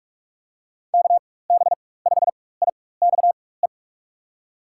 Morse code, heard as a keyed tone, about 46 wpm, spaced as for 21 wpm, 700 Hz. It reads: KB5IXE
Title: USA Call Signs - 6 Characters - 4x Character Spacing - 45wpm